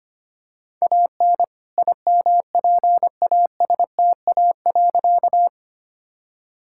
Morse code, heard as a keyed tone, about 25 wpm, 700 Hz. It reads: AN IMPASTA.
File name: practice / 25wpm / All-Age Appropriate Jokes - 25wpm